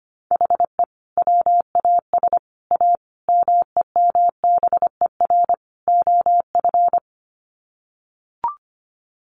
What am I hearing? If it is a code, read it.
Morse code, heard as a keyed tone, about 25 wpm, 700 Hz. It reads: HE WAS A MEMBER OF
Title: Sets of 5 Words Encore2 25wpm